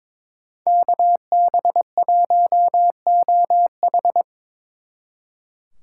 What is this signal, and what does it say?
Morse code, heard as a keyed tone, about 25 wpm, 700 Hz. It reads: KB1OH
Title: Speed Racing - USA 2x2 Call Signs 22wpm